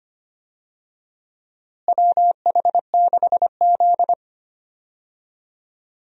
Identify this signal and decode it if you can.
Morse code, heard as a keyed tone, about 25 wpm, 700 Hz. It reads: WH6Z
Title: USA Call Signs - 4 Characters - Repeated 2x - 25wpm